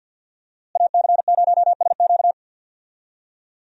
Morse code, heard as a keyed tone, about 52 wpm, 700 Hz. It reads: AC0SQ
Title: USA Call Signs - 5 Characters - Repeated 3x - 50wpm